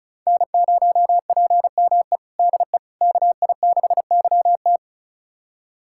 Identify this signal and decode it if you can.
Morse code, heard as a keyed tone, about 36 wpm, 690 Hz. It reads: N0PME DE KI6YT
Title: USA Callsign DE Callsign - 5 Characters - 35wpm